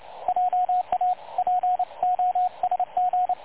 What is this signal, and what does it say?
Morse code, heard as a keyed tone, about 29 wpm, 730 Hz, with radio noise. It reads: JA POSG
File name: dlazaawansowanych / nagraniazpasma / websdr recording 3565.0kHz